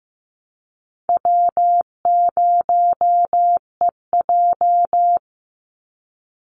Morse code, heard as a keyed tone, about 15 wpm, 700 Hz. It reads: W0EJ
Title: Easy Call Signs - 15wpm